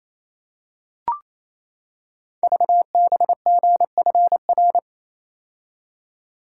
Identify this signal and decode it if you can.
Morse code, heard as a keyed tone, about 27 wpm, 700 Hz. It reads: VBGFR
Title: Code Groups - 5 Letters - 28wpm